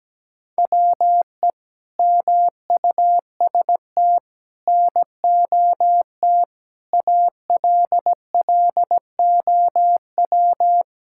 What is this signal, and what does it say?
Morse code, heard as a keyed tone, about 17 wpm, 700 Hz. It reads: WE MUST NOT ALLOW
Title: Sentences from Top 1000 Words - Review Part 5 - 17wpm